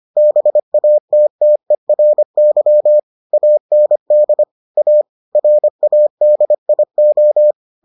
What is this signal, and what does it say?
Morse code, heard as a keyed tone, about 25 wpm, 600 Hz. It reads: BATTERY AND A RADIO